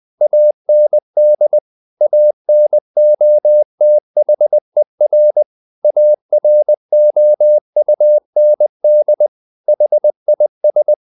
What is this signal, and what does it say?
Morse code, heard as a keyed tone, about 20 wpm, 600 Hz. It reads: AND ANOTHER AROUND HIS
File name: the old man and the sea part 2 20wpm